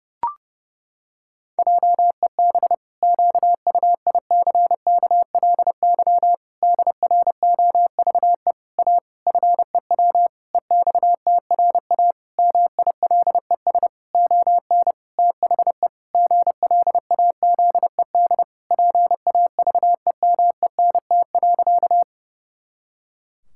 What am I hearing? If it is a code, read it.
Morse code, heard as a keyed tone, about 30 wpm, 700 Hz. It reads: JEB QUICKLY DROVE A FEW EXTRA MILES ON THE GLAZED PAVEMENT.
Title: Pangram Challenge v2 - 30wpm